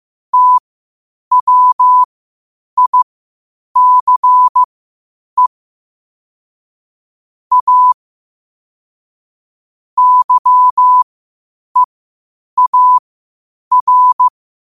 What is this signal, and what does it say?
Morse code, heard as a keyed tone, about 15 wpm, spaced as for 8 wpm, 1000 Hz. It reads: TWICE A YEAR
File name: qotdfiles / 8wpm